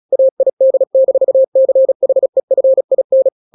Morse code, heard as a keyed tone, about 36 wpm, 530 Hz. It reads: AID-CHEFIN
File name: daily / 35wpm